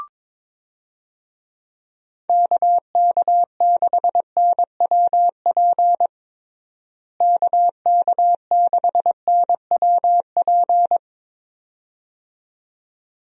Morse code, heard as a keyed tone, about 22 wpm, 700 Hz. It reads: KK6NWP KK6NWP
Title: USA Call Signs - 6 Characters - Repeated 2x - 22wpm